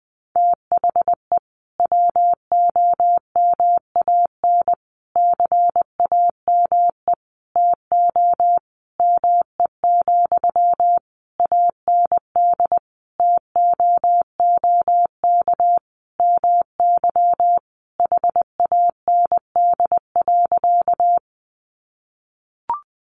Morse code, heard as a keyed tone, about 20 wpm, 700 Hz. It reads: THE WOMAN CAME TO ME, AND TOOK MY HAND.
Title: Sentences from Top 1100 Words 20wpm